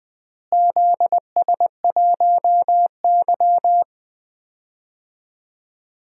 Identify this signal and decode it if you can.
Morse code, heard as a keyed tone, about 20 wpm, 700 Hz. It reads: ZS1Y
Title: DXCC-100 Call Signs - 4 Characters - Repeated 2x - 20wpm